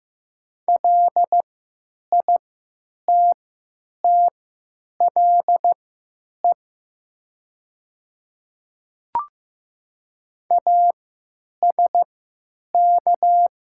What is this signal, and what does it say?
Morse code, heard as a keyed tone, about 15 wpm, spaced as for 9 wpm, 700 Hz. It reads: LITTLE ASK
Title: 3x Character Spacing - Top 200 Words - 15wpm